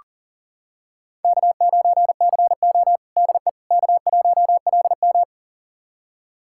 Morse code, heard as a keyed tone, about 37 wpm, 700 Hz. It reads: K9CO DE K1LM
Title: USA Callsign DE Callsign - 4 Characters - 40wpm